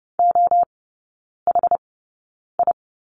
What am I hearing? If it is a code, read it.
Morse code, heard as a keyed tone, about 30 wpm, 700 Hz. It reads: O H I
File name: Lesson 19e - Letter w - Three Characters 30wpm